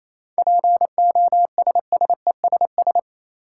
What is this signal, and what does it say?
Morse code, heard as a keyed tone, about 27 wpm, 700 Hz. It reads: POSSESS
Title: Seven Letter Words - 28wpm